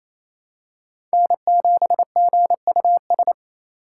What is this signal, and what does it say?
Morse code, heard as a keyed tone, about 28 wpm, 700 Hz. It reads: N7GUS
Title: USA Call Signs - 5 Characters - 4x Character Spacing - 28wpm